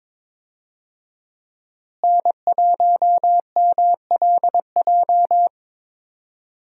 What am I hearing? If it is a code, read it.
Morse code, heard as a keyed tone, about 22 wpm, 700 Hz. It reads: N1MLJ
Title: USA Call Signs - 5 Characters - Repeated 3x - 22wpm